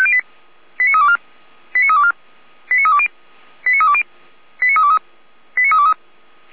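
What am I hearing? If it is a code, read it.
Five-tone selective call (ZVEI), heard as quick runs of five tones, tones of about 70 ms. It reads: 89324 89324 89320 89320 89323 89323
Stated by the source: ZVEI five-tone selective call received on an SDR.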